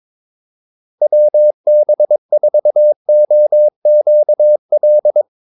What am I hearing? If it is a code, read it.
Morse code, heard as a keyed tone, about 22 wpm, 600 Hz. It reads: WB4OQL
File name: Callsigns-22wpm